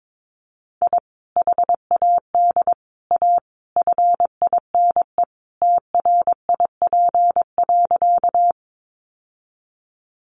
Morse code, heard as a keyed tone, about 22 wpm, 700 Hz. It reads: I HAD A FINE TRIP.
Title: Sentences from Top 1300 Words 22wpm